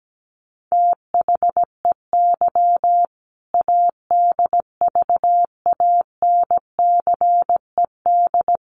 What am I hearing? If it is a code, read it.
Morse code, heard as a keyed tone, about 17 wpm, 700 Hz. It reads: THEY ADVANCED
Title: Sentences from Top 1200 Words 17wpm